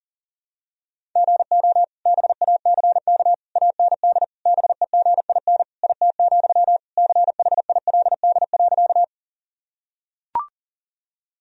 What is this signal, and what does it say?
Morse code, heard as a keyed tone, about 40 wpm, 700 Hz. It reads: GO BACK AND BEGIN IT, CHILD.
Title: Sentences from Top 1000 Words - Review Part 1 - 40wpm